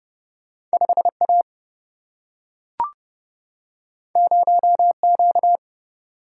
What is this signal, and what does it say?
Morse code, heard as a keyed tone, about 30 wpm, 700 Hz. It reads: 5 A 0 Q
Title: Single-Letter Single-Number 30wpm